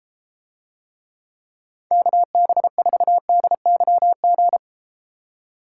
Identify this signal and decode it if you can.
Morse code, heard as a keyed tone, about 32 wpm, 700 Hz. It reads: KB4DYG